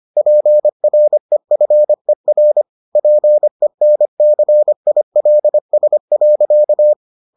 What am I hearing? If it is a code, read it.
Morse code, heard as a keyed tone, about 25 wpm, 600 Hz. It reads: PREFER PENCILS.